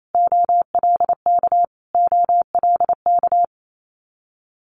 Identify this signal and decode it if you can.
Morse code, heard as a keyed tone, about 28 wpm, 700 Hz. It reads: OLK OLK
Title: Multi-letter Phonemes Repeated 3x 28wpm